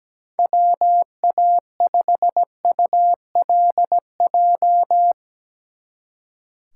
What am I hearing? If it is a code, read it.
Morse code, heard as a keyed tone, about 17 wpm, 700 Hz. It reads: WA5ULJ
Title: Speed Racing - USA 2x3 Call Signs 17wpm